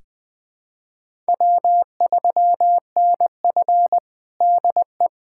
Morse code, heard as a keyed tone, about 20 wpm, 700 Hz. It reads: W3NF DE
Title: USA Callsign DE Callsign - 4 Characters - 20wpm